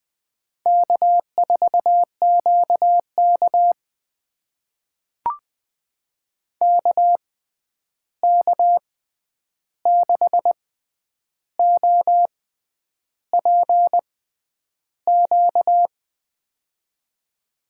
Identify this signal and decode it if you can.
Morse code, heard as a keyed tone, about 20 wpm, spaced as for 9 wpm, 700 Hz. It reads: K4QK KK6OPQ
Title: USA Call Signs - 6x Character Spacing - 20wpm